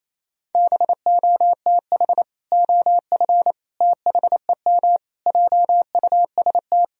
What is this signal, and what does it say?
Morse code, heard as a keyed tone, about 28 wpm, 700 Hz. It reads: BOTH OF THEM JUST
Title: Sentences from Top 1000 Words - Review Part 2 - 28wpm